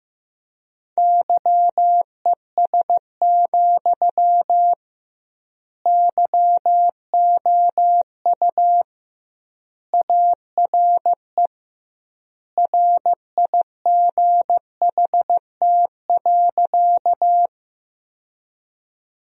Morse code, heard as a keyed tone, about 15 wpm, 700 Hz. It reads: YES, YOU ARE RIGHT.
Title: 2x Word-Spacing - Sentences from Top 500 Words - 15wpm